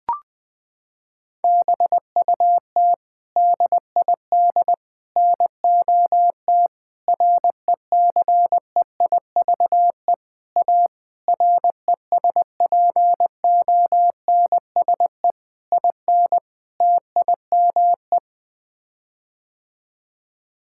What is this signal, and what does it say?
Morse code, heard as a keyed tone, about 20 wpm, 700 Hz. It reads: BUT DID NOT RECEIVE A RESPONSE IN TIME
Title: Sets of 8 Words - 20wpm